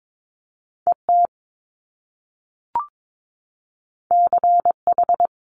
Morse code, heard as a keyed tone, about 22 wpm, 700 Hz. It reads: ET CH
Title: Top Two Letter Combinations in Words 22wpm